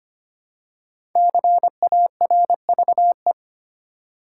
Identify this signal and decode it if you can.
Morse code, heard as a keyed tone, about 25 wpm, 700 Hz. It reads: CARVE
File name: 5 Letter Words - First 3 Letters Given - 25wpm